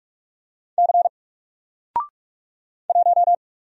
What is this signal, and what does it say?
Morse code, heard as a keyed tone, about 46 wpm, 700 Hz. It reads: C 1
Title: Single Letter-Number 45wpm